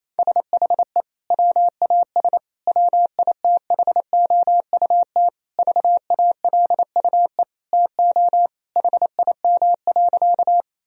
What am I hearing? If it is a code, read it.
Morse code, heard as a keyed tone, about 28 wpm, 700 Hz. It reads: SHE WAS WITHOUT VALUE TO HIM.
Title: Sentences from Top 1000 Words - Review Part 3 - 28wpm